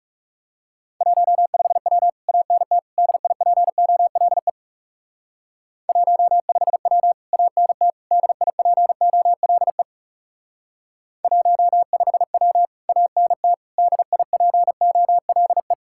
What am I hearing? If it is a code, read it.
Morse code, heard as a keyed tone, about 36 wpm, 700 Hz. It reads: 15W ANT DIPOLE 15W ANT DIPOLE 15W ANT DIPOLE
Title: Speed-Racing Common QSO Phrases - 30wpm